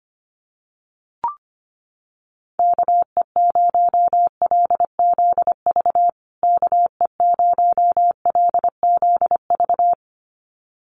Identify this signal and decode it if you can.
Morse code, heard as a keyed tone, about 25 wpm, 700 Hz. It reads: KE0LZV KE0LZV
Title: USA Call Signs Repeated 2x 25wpm